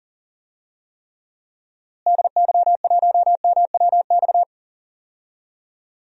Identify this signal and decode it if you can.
Morse code, heard as a keyed tone, about 40 wpm, 700 Hz. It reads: DY1MWX
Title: DXCC-100 Call Signs - 6 Characters - Repeated 3x - 40wpm